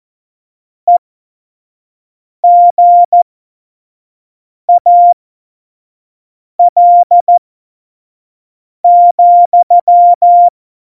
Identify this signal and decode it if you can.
Morse code, heard as a keyed tone, about 14 wpm, spaced as for 5 wpm, 700 Hz. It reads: EGAL,
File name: Morsekurs / Witze / Witz 14. November 2017, 25 F70